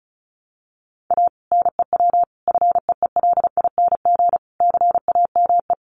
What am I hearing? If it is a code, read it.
Morse code, heard as a keyed tone, about 34 wpm, 700 Hz. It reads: A NEW FEELING CAME
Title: Sentences from Top 1200 Words 35wpm